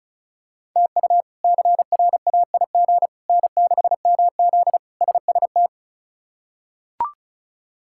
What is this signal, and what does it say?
Morse code, heard as a keyed tone, about 36 wpm, 700 Hz. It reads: TU CRAIG N6MZ SST
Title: K1USN SST QSO Practice - 35wpm